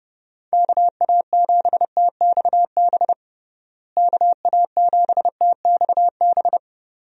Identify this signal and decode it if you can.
Morse code, heard as a keyed tone, about 30 wpm, 700 Hz. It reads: KA7TXB KA7TXB
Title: USA Call Signs - 6 Characters - Repeated 2x - 30wpm